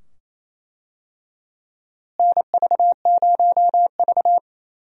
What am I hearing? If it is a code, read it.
Morse code, heard as a keyed tone, about 27 wpm, 700 Hz. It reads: NV0V
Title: USA Call Signs - 4 Characters - Repeated 2x - 28wpm